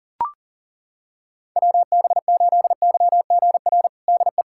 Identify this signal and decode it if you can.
Morse code, heard as a keyed tone, about 40 wpm, 700 Hz. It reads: WB8YGR DE
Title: USA Callsign DE Callsign - 6 Characters - 40wpm